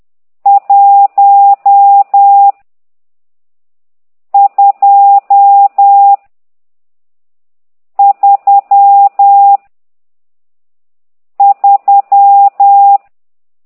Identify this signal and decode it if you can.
Morse code, heard as a keyed tone, about 10 wpm, 800 Hz. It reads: 1 2 3 3
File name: ct2idw / exerc12